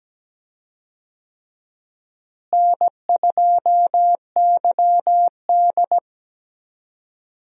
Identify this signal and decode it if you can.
Morse code, heard as a keyed tone, about 17 wpm, 700 Hz. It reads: N2YD
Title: USA Call Signs - 4 Characters - Repeated 3x - 17wpm